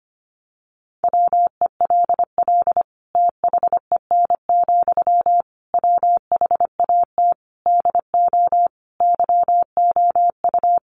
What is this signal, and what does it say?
Morse code, heard as a keyed tone, about 25 wpm, 700 Hz. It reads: WELL THEN, WHAT DO YOU
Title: Sentences from Top 200 Words 25wpm